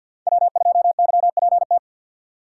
Morse code, heard as a keyed tone, about 47 wpm, 700 Hz. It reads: W2YPT
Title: USA Call Signs - 5 Characters - 2x Character Spacing - 50wpm